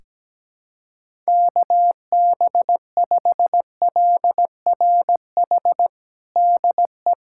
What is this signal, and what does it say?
Morse code, heard as a keyed tone, about 17 wpm, 700 Hz. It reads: KB5LRH DE
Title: USA Callsign DE Callsign 17wpm